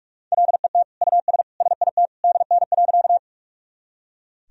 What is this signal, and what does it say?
Morse code, heard as a keyed tone, about 41 wpm, 700 Hz. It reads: LET US SIT DN.
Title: Sentences from Top 500 Words and Abbreviations 45wpm